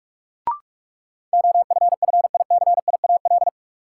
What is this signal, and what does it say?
Morse code, heard as a keyed tone, about 47 wpm, 700 Hz. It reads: OFFICIAL
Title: Sentences from Top 1000 Words - New Word Drill - 45wpm